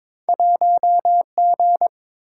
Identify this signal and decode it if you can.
Morse code, heard as a keyed tone, about 30 wpm, 700 Hz. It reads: 1G